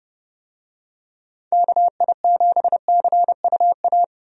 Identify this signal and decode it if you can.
Morse code, heard as a keyed tone, about 30 wpm, 700 Hz. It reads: KI7CUA
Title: USA Call Signs - 6 Characters - Repeated 2x - 30wpm